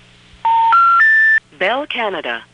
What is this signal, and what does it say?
Special information tone, with speech.
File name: phonetrips / Nantes1HQ